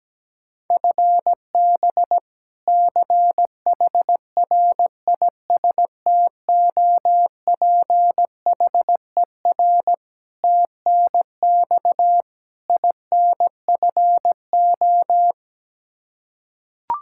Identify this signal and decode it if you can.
Morse code, heard as a keyed tone, about 17 wpm, 700 Hz. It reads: FB CHRISTOPHER TNX INFO
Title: K1USN SST QSO Practice 17wpm